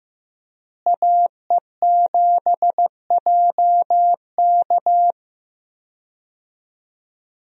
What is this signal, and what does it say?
Morse code, heard as a keyed tone, about 15 wpm, 700 Hz. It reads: AE7JK
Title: USA Call Signs - 5 Characters - Repeated 2x - 15wpm